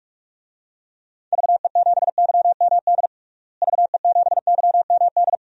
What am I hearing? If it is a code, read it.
Morse code, heard as a keyed tone, about 46 wpm, 700 Hz. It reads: VE7YMD VE7YMD
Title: DXCC-100 Call Signs - 6 Characters - Repeated 2x - 45wpm